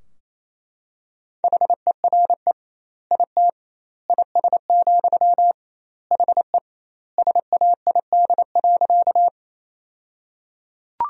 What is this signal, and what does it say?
Morse code, heard as a keyed tone, about 28 wpm, 700 Hz. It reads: HERE IT IS, HE SAID.
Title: 2x Word-Spacing - Sentences from Top 200 Words - 28wpm